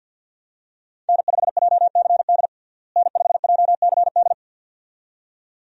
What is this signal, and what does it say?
Morse code, heard as a keyed tone, about 53 wpm, 700 Hz. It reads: N5JCD N5JCD
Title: USA Call Signs - 5 Characters - Repeated 2x - 50wpm